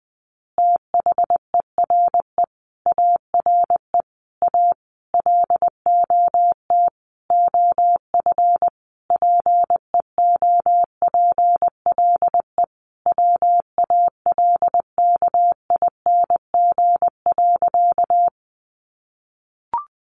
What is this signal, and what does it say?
Morse code, heard as a keyed tone, about 20 wpm, 700 Hz. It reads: THERE ARE A LOT OF PEOPLE WALKING.
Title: Easy Phrases 20wpm